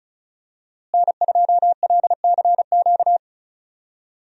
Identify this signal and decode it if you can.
Morse code, heard as a keyed tone, about 34 wpm, 700 Hz. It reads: N2LCQ
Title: USA Call Signs - 5 Characters - Repeated 2x - 35wpm